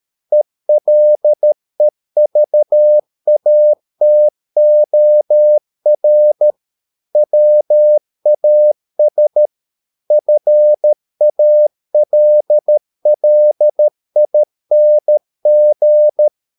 Morse code, heard as a keyed tone, about 13 wpm, 600 Hz. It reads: ELEVATOR WAS FALLING